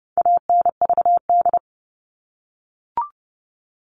Morse code, heard as a keyed tone, about 30 wpm, 700 Hz. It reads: ANVD
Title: All Cut Numbers - 4-Digit 30wpm